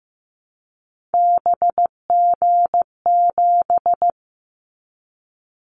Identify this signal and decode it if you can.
Morse code, heard as a keyed tone, about 15 wpm, 700 Hz. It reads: BG7
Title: Callsign Prefixes for Top 100 DXCC - 15wpm